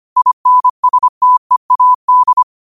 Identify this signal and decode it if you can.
Morse code, heard as a keyed tone, about 25 wpm, 1000 Hz. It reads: INSTEAD
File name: qotdfiles / 25wpm